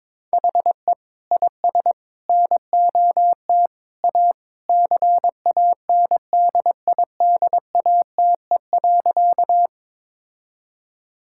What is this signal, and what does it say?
Morse code, heard as a keyed tone, about 22 wpm, 700 Hz. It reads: HE IS NOT A CANDIDATE.
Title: Sentences from Top 1600 Words 22wpm